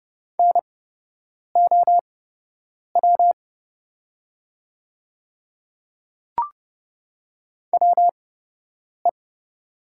Morse code, heard as a keyed tone, about 30 wpm, spaced as for 8 wpm, 700 Hz. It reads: NOW WE
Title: 8x Character Spacing - Top 100 Words - 30wpm